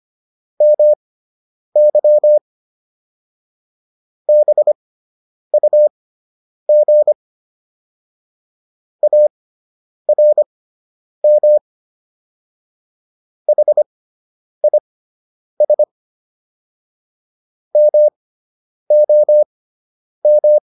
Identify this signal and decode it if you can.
Morse code, heard as a keyed tone, about 25 wpm, spaced as for 9 wpm, 600 Hz. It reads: MY BUG ARM HIS MOM